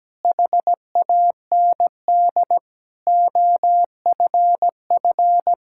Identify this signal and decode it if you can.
Morse code, heard as a keyed tone, about 17 wpm, 700 Hz. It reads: HAND OFF
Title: Sentences from Top 1000 Words - Review Part 5 - 17wpm